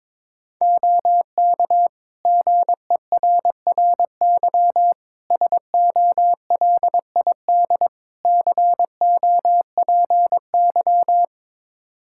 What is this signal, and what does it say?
Morse code, heard as a keyed tone, about 22 wpm, 700 Hz. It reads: OK GERRY SOLID COPY
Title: Practice QSO - Exchange 3 - More Info - 22wpm